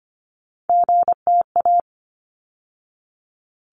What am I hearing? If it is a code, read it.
Morse code, heard as a keyed tone, about 25 wpm, 700 Hz. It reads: GTA
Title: ARRL Sections 25wpm